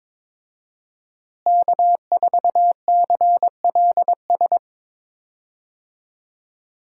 Morse code, heard as a keyed tone, about 22 wpm, 700 Hz. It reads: K4CLS
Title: USA Call Signs - 5 Characters - Repeated 2x - 22wpm